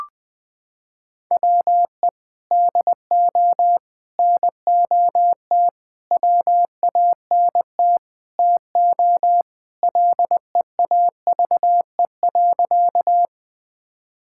Morse code, heard as a keyed tone, about 21 wpm, 700 Hz. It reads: WE DO NOT WANT TO LEAVE.
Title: Sentences with 5-Word Phrases - 20wpm